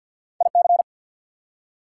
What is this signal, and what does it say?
Morse code, heard as a keyed tone, about 54 wpm, 700 Hz. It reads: IC
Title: Prefixes and Suffixes 50wpm